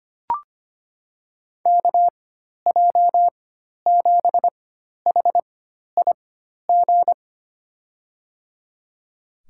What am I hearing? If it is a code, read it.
Morse code, heard as a keyed tone, about 25 wpm, spaced as for 12 wpm, 700 Hz. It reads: KJ7HIG
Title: USA Call Signs - 6 Characters - 4x Character Spacing - 25wpm